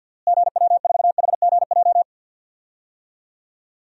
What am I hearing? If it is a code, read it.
Morse code, heard as a keyed tone, about 51 wpm, 700 Hz. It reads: GW4HGJ
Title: DXCC-100 Call Signs - 6 Characters - Repeated 2x - 50wpm